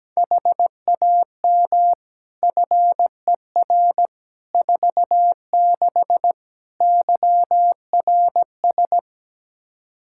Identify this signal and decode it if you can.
Morse code, heard as a keyed tone, about 17 wpm, 700 Hz. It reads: HAM FER 46 YRS
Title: Practice QSO - Exchange 3 - More Info - 17wpm